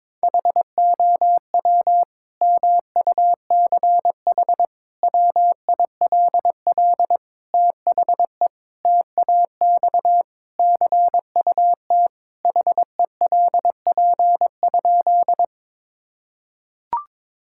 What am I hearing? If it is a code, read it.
Morse code, heard as a keyed tone, about 21 wpm, 700 Hz. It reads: HOW MUCH WILL THE TAX CUT HELP?
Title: Sentences from Top 1000 Words - Review Part 5 - 22wpm